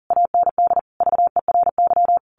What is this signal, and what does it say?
Morse code, heard as a keyed tone, about 40 wpm, 700 Hz. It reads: AND VERY
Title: A Sound Of Thunder 40wpm